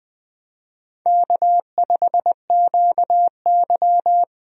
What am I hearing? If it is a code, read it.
Morse code, heard as a keyed tone, about 20 wpm, 700 Hz. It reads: K5QY
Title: USA Call Signs - 4 Characters - 20wpm